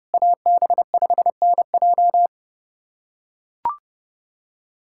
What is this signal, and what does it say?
Morse code, heard as a keyed tone, about 32 wpm, 700 Hz. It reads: AB5NJ
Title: USA Callsign DE Callsign - 5 Characters - 30wpm